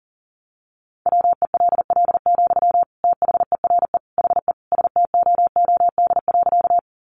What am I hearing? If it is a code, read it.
Morse code, heard as a keyed tone, about 40 wpm, 700 Hz. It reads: WELL, THERE HE STOOD.
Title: Sentences from Top 500 Words 40wpm